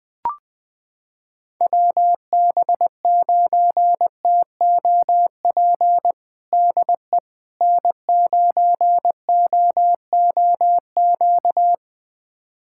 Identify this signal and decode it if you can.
Morse code, heard as a keyed tone, about 20 wpm, 700 Hz. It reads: WB9TOP DE N9OOQ
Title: USA Callsign DE Callsign 20wpm